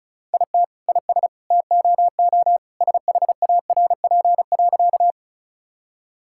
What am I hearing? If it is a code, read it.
Morse code, heard as a keyed tone, about 34 wpm, 700 Hz. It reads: IT IS TOO SHARP.
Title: Sentences from Top 1600 Words 35wpm